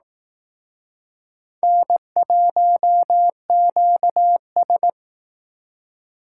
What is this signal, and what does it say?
Morse code, heard as a keyed tone, about 18 wpm, 700 Hz. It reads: N1QS
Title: Speed Racing - USA 1x2 Call Signs 15wpm